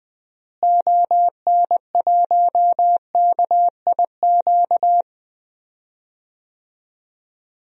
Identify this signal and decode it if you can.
Morse code, heard as a keyed tone, about 21 wpm, 700 Hz. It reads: ON1KIQ
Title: DXCC-100 Call Signs - 6 Characters - Repeated 2x - 20wpm